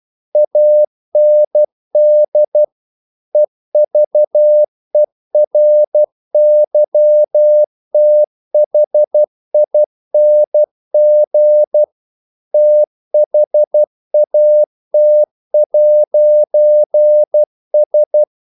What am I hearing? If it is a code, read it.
Morse code, heard as a keyed tone, about 12 wpm, 600 Hz. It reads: AND EVERYTHING THAT'S
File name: CW / Practice / If12